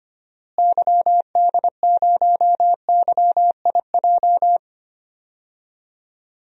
Morse code, heard as a keyed tone, about 25 wpm, 700 Hz. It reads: YD0YIJ
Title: DXCC-100 Call Signs - 6 Characters - Repeated 3x - 25wpm